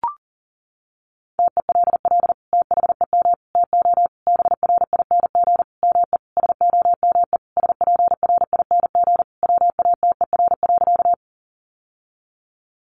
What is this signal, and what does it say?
Morse code, heard as a keyed tone, about 37 wpm, 700 Hz. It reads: TELL THEM TO BRING ME SOME SPRING WATER.
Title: Sentences from Top 1200 Words 40wpm